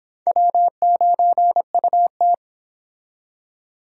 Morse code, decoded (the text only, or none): W9UT